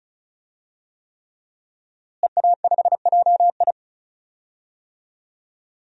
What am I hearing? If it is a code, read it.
Morse code, heard as a keyed tone, about 36 wpm, 700 Hz. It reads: EA5JI
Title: DXCC-100 Call Signs - 5 Characters - Repeated 3x - 35wpm